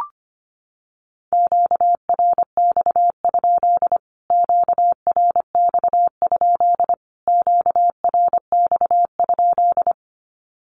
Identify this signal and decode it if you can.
Morse code, heard as a keyed tone, about 25 wpm, 700 Hz. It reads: QRX? QRX? QRX?
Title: Q-Codes Repeated 3x 25wpm